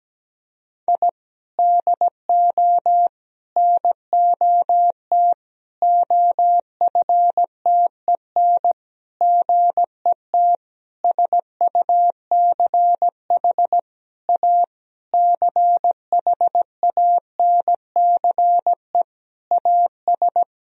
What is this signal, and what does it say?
Morse code, heard as a keyed tone, about 17 wpm, 700 Hz. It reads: I DO NOT OFTEN GET SUCH A CHANCE AS
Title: Sentences from Top 1000 Words - Review Part 1 - 17wpm